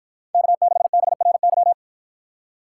Morse code, heard as a keyed tone, about 51 wpm, 700 Hz. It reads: K6BRY